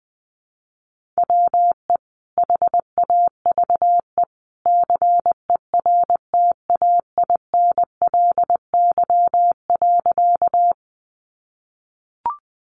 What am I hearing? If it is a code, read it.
Morse code, heard as a keyed tone, about 20 wpm, 700 Hz. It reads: WE HAVE CERTAINLY.
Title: Sentences from Top 900 Words 20wpm